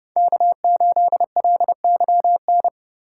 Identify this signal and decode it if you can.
Morse code, heard as a keyed tone, about 30 wpm, 700 Hz. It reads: K8LYN